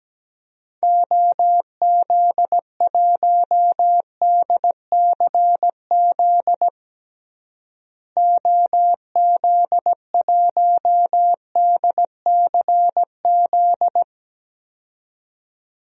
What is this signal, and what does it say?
Morse code, heard as a keyed tone, about 17 wpm, 700 Hz. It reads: OZ1DCZ OZ1DCZ
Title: DXCC-100 Call Signs - 6 Characters - Repeated 2x - 17wpm